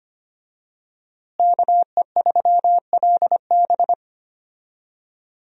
Morse code, heard as a keyed tone, about 25 wpm, 700 Hz. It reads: KE3LB
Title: USA Call Signs - 5 Characters - Repeated 3x - 25wpm